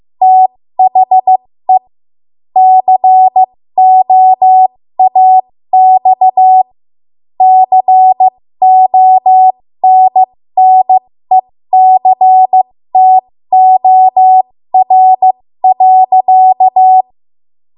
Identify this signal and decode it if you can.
Morse code, heard as a keyed tone, about 15 wpm, spaced as for 13 wpm, 750 Hz. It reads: THE COAX CONNECTOR.
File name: Morse / 13WPM